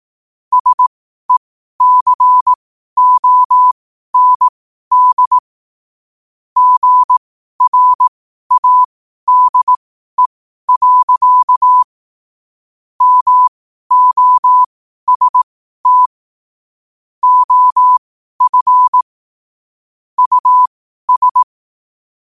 Morse code, heard as a keyed tone, about 18 wpm, spaced as for 12 wpm, 1000 Hz. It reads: SECOND GRADE. MOST OF US
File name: qotdfiles / 12wpm